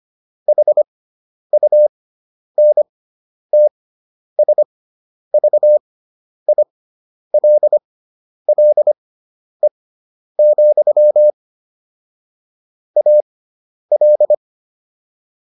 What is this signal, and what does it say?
Morse code, heard as a keyed tone, about 25 wpm, spaced as for 10 wpm, 600 Hz. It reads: HUNTSVILLE, AL